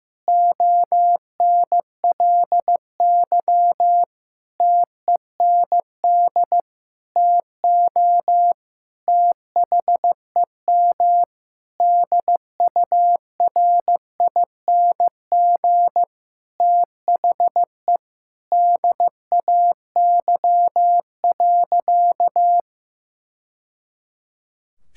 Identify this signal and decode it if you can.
Morse code, heard as a keyed tone, about 15 wpm, 700 Hz. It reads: ONLY TEND TO THEM DURING THE DAY.
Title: Sentences from Top 1000 Words - Review Part 5 - 15wpm